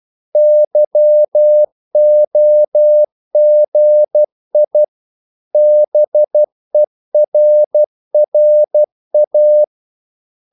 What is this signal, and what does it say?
Morse code, heard as a keyed tone, about 12 wpm, 600 Hz. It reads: YOGI BERRA
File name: CW / Practice / EnglishQuotations0312